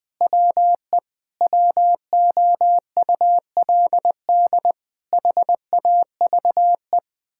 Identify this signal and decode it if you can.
Morse code, heard as a keyed tone, about 20 wpm, 700 Hz. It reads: WE WOULD HAVE
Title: Sentences with 7-Word Phrases - 20wpm